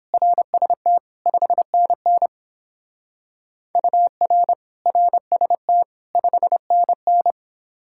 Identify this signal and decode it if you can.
Morse code, heard as a keyed tone, about 28 wpm, 700 Hz. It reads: RST 5NN UR RST 5NN